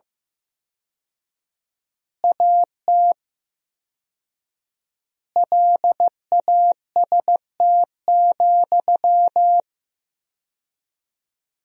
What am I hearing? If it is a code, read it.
Morse code, heard as a keyed tone, about 15 wpm, 700 Hz. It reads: AT LAST,
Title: 4x Word-Spacing - Sentences from Top 300 Words - 15wpm